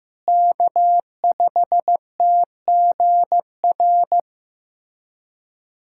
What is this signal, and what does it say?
Morse code, heard as a keyed tone, about 16 wpm, 700 Hz. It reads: K5TGR
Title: USA Callsign DE Callsign - 5 Characters - 15wpm